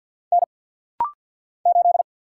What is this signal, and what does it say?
Morse code, heard as a keyed tone, about 50 wpm, 700 Hz. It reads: N 8
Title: Single Letter-Number Alt - Rapid-Fire 50wpm